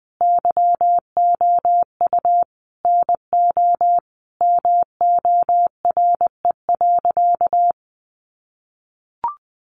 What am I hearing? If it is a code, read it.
Morse code, heard as a keyed tone, about 19 wpm, 700 Hz. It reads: YOU NO MORE.
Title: Sentences from Top 1200 Words 20wpm